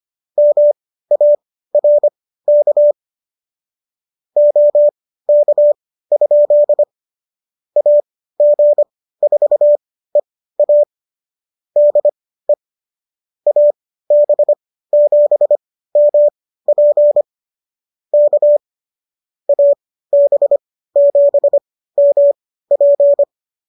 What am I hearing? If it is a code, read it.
Morse code, heard as a keyed tone, about 25 wpm, spaced as for 15 wpm, 590 Hz. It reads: MARK OK? AG4EA DE AB7MP K AB7MP